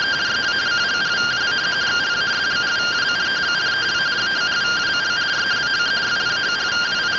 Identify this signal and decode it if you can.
Two-tone teleprinter signal, about 170 Hz shift.